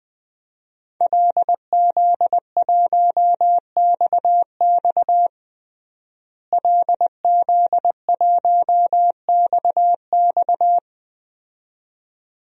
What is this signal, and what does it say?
Morse code, heard as a keyed tone, about 21 wpm, 700 Hz. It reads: LZ1XX LZ1XX
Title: DXCC-100 Call Signs - 5 Characters - Repeated 2x - 20wpm